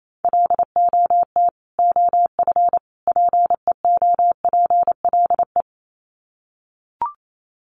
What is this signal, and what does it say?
Morse code, heard as a keyed tone, about 28 wpm, 700 Hz. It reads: LOT OF PEOPLE
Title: Sets of 3 Words 28wpm